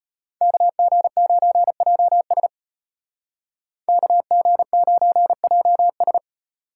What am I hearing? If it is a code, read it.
Morse code, heard as a keyed tone, about 34 wpm, 700 Hz. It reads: KG9JS KG9JS